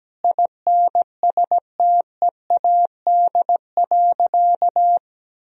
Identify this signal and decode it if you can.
Morse code, heard as a keyed tone, about 17 wpm, 700 Hz. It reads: INSTEAD.
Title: Sentences from Top 1000 Words - Review Part 4 - 17wpm